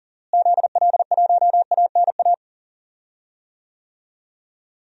Morse code, heard as a keyed tone, about 40 wpm, 700 Hz. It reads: ZL1ANA
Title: DXCC-100 Call Signs - 6 Characters - Repeated 3x - 40wpm